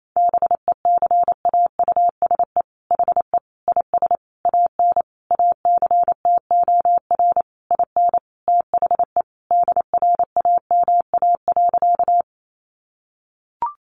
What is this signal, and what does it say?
Morse code, heard as a keyed tone, about 28 wpm, 700 Hz. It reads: BECAUSE HE IS AN ACTOR IN THE DRAMA.
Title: Sentences from Top 1500 Words 28wpm